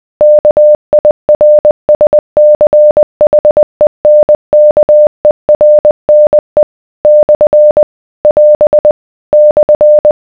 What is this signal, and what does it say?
Morse code, heard as a keyed tone, about 20 wpm, 600 Hz. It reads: KIRSCHENKERNE / <AS> /